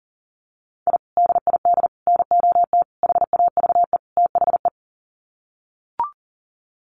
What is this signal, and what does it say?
Morse code, heard as a keyed tone, about 40 wpm, 700 Hz. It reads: I DID NOT HAVE THE
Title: Sets of 5 Words Encore 40wpm